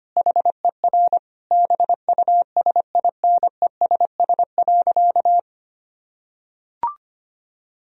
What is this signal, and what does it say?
Morse code, heard as a keyed tone, about 25 wpm, 700 Hz. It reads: HER BUSINESS.
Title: Sentences from Top 1000 Words - Review Part 1 - 25wpm